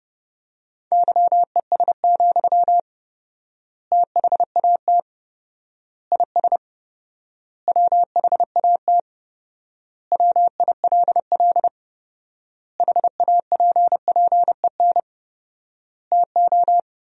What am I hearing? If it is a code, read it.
Morse code, heard as a keyed tone, about 30 wpm, 700 Hz. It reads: YES, THAT IS WHAT WILL HAPPEN TO
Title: 4x Word-Spacing - Sentences from Top 500 Words - 30wpm